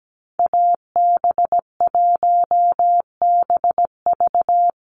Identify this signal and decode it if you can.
Morse code, heard as a keyed tone, about 17 wpm, 700 Hz. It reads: AB1BV
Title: USA Call Signs 17wpm